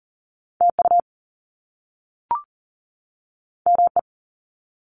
Morse code, heard as a keyed tone, about 40 wpm, 700 Hz. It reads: TU ME
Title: Top Two Letter Combinations in Words 40wpm